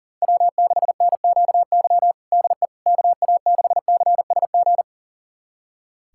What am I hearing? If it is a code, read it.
Morse code, heard as a keyed tone, about 37 wpm, 700 Hz. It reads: W6NQY DE KA6CSG
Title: USA Callsign DE Callsign 40wpm